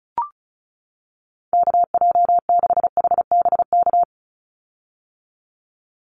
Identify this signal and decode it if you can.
Morse code, heard as a keyed tone, about 34 wpm, 700 Hz. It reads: KJ6HBK
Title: USA Call Signs 35wpm